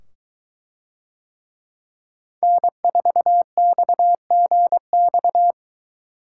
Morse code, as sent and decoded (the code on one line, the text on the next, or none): -. ....- -..- --. -..-
N4XGX